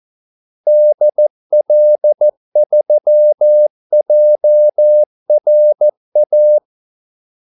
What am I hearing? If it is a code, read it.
Morse code, heard as a keyed tone, about 14 wpm, 600 Hz. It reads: DL3JRA
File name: Callsigns-14wpm